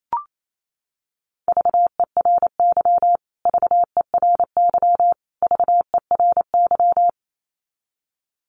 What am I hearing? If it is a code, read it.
Morse code, heard as a keyed tone, about 28 wpm, 700 Hz. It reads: VERY VERY VERY
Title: Top 200 Words Repeated 28wpm